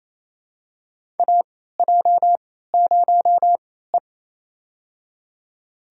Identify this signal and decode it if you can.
Morse code, heard as a keyed tone, about 28 wpm, spaced as for 16 wpm, 700 Hz. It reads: AJ0E
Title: USA Call Signs - 4 Characters - 3x Character Spacing - 28wpm